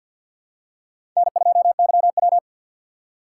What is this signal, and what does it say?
Morse code, heard as a keyed tone, about 50 wpm, 700 Hz. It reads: N2YW
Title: USA Call Signs - 4 Characters - 6x Character Spacing - 50wpm